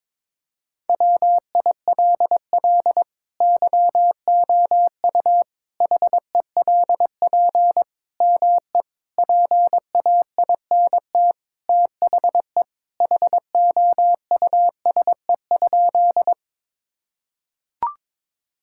Morse code, heard as a keyed tone, about 22 wpm, 700 Hz. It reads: WILL YOU HELP ME PAINT THE HOUSE?
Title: Sentences from Top 1000 Words - Review Part 6 - 22wpm